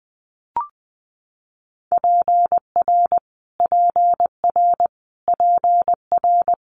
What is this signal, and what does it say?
Morse code, heard as a keyed tone, about 20 wpm, 700 Hz. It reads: PR PR PR
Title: Multi-letter Phonemes Repeated 3x 20wpm